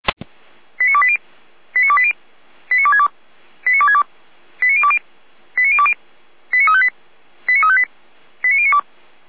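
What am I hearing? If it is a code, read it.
Five-tone selective call (ZVEI), heard as quick runs of five tones, tones of about 70 ms. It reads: 89280 89280 89262 89262 89020 89020 89358 89358 89092
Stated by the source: ZVEI five-tone alert received by SDR.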